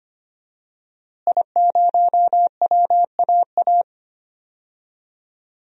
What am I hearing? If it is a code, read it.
Morse code, heard as a keyed tone, about 25 wpm, 700 Hz. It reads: I0WAA
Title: DXCC-100 Call Signs - 5 Characters - Repeated 2x - 25wpm